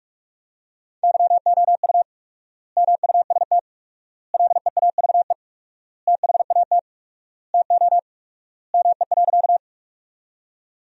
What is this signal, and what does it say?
Morse code, heard as a keyed tone, about 46 wpm, 700 Hz. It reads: YOU MUST LEAVE THAT TO ME.
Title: 4x Word-Spacing - Sentences from Top 400 Words - 45wpm